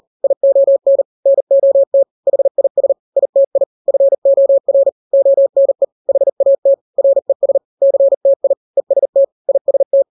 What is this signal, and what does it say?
Morse code, heard as a keyed tone, about 40 wpm, 550 Hz. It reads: ION NOT HIS ITI FOR ONE HAT RES CTI EST IST
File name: n-grams / 3-grams 40 35 WPM